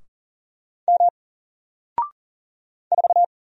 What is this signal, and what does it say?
Morse code, heard as a keyed tone, about 40 wpm, 700 Hz. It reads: M 4
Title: Single Character (Letter:Number:Punctuation:Prosign) Alt 40wpm